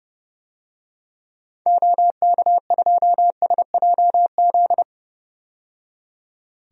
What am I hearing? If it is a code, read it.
Morse code, heard as a keyed tone, about 30 wpm, 700 Hz. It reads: OK2SJZ
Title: DXCC-100 Call Signs - 6 Characters - 30wpm